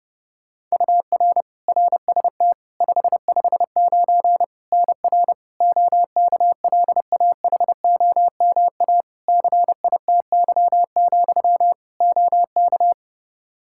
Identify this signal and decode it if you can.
Morse code, heard as a keyed tone, about 30 wpm, 700 Hz. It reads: UR RST 559 NR OKLAHOMA CITY, OK
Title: Practice QSO - Exchange 1 - Beginning - 30wpm